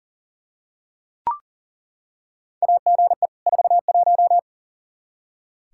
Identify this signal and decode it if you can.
Morse code, heard as a keyed tone, about 40 wpm, 700 Hz. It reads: AGE 41
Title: Practice QSO - Exchange 3 - More Info - 40wpm